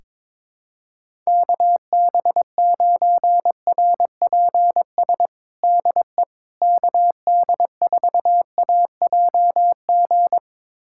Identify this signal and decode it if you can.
Morse code, heard as a keyed tone, about 22 wpm, 700 Hz. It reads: KB9RPS DE KD4AJG
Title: USA Callsign DE Callsign - 6 Characters - 22wpm